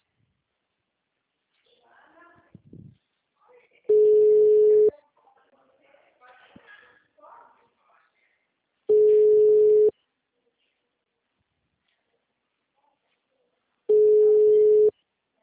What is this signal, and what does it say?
Ringback tone (Europe, 425 Hz), with speech.
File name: out- -2022- - - .2917